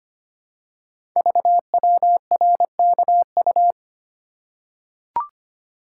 Morse code, heard as a keyed tone, about 25 wpm, 700 Hz. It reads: VWRKU
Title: Code Groups - 5 Letter:Numbers - 25wpm